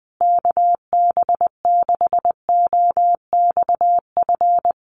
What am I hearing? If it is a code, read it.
Morse code, heard as a keyed tone, about 20 wpm, 700 Hz. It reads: KB6OXF
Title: USA Call Signs Repeated 2x 20wpm